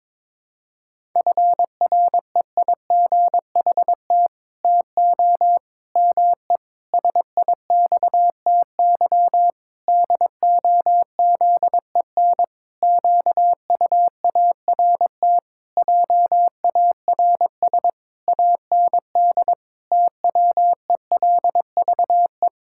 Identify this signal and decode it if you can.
Morse code, heard as a keyed tone, about 22 wpm, 700 Hz. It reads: FREIGHT TO ME SIXTY DOZEN QUART JARS AND TWELVE
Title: Pangram Challenge v2 - 22wpm